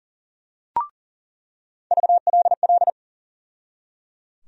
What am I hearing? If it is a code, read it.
Morse code, heard as a keyed tone, about 38 wpm, 700 Hz. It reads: VLL
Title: USA Call Sign Postfixes - 3 Letters 40wpm